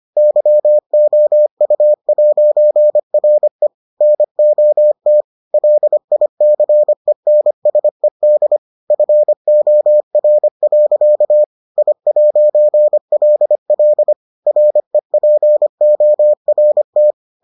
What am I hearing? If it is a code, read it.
Morse code, heard as a keyed tone, about 25 wpm, 600 Hz. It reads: YOU'RE NOT LICENSED FOR. I'LL REPORT